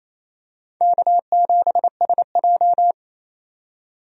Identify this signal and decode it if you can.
Morse code, heard as a keyed tone, about 28 wpm, 700 Hz. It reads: K7SJ
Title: USA Call Signs - 4 Characters - 28wpm